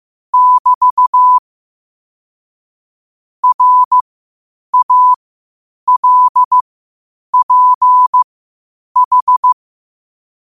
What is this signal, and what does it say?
Morse code, heard as a keyed tone, about 15 wpm, spaced as for 8 wpm, 1000 Hz. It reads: <BT> RALPH
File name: qotdfiles / 8wpm